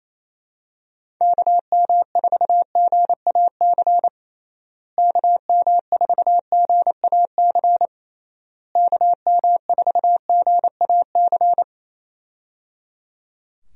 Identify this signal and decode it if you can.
Morse code, heard as a keyed tone, about 27 wpm, 700 Hz. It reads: KM4GAC KM4GAC KM4GAC
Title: USA Call Signs - 6 Characters - Repeated 3x - 28wpm